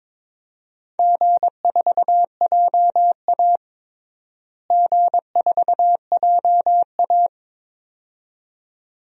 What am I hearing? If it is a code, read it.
Morse code, heard as a keyed tone, about 22 wpm, 700 Hz. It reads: G4JA G4JA
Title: DXCC-100 Call Signs - 4 Characters - Repeated 2x - 22wpm